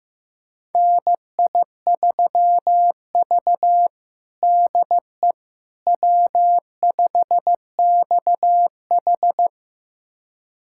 Morse code, heard as a keyed tone, about 15 wpm, 700 Hz. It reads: NI3V DE W5XH
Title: USA Callsign DE Callsign - 4 Characters - 15wpm